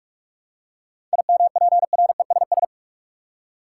Morse code, heard as a keyed tone, about 47 wpm, 700 Hz. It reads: IMPRESS
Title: Seven Letter Words - 45wpm